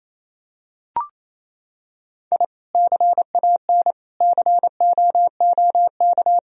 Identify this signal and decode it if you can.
Morse code, heard as a keyed tone, about 27 wpm, 700 Hz. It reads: I CAN COOK
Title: Sentences from Top 1000 Words - Review Part 4 - 28wpm